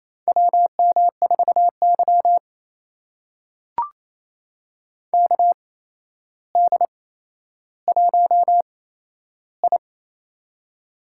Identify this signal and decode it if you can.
Morse code, heard as a keyed tone, about 28 wpm, spaced as for 12 wpm, 700 Hz. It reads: WM4Y KD1I
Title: USA Call Signs - 4 Characters - 8x Character Spacing - 28wpm